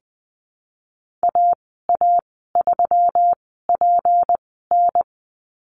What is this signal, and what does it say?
Morse code, heard as a keyed tone, about 20 wpm, spaced as for 15 wpm, 700 Hz. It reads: AA3PN
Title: 179 Lesson 29g - Slash - Call Signs 20wpm